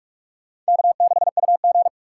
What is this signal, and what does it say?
Morse code, heard as a keyed tone, about 61 wpm, spaced as for 53 wpm, 700 Hz. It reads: K6UG
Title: Speed Racing - USA Call Signs 45wpm